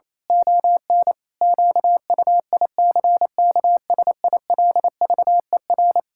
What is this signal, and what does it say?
Morse code, heard as a keyed tone, about 28 wpm, 700 Hz. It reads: ON QUICKSILVER
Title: Pangram Challenge v2 - 28wpm